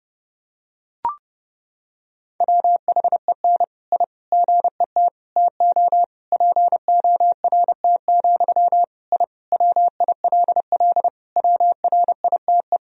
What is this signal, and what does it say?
Morse code, heard as a keyed tone, about 30 wpm, 700 Hz. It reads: WHEN I GET TO PORT, I WILL WRITE
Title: Sentences from Top 1000 Words - Review Part 1 - 30wpm